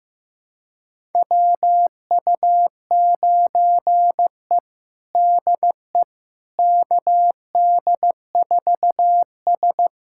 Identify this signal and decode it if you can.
Morse code, heard as a keyed tone, about 15 wpm, 700 Hz. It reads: WU9E DE KD4S
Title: USA Callsign DE Callsign - 4 Characters - 15wpm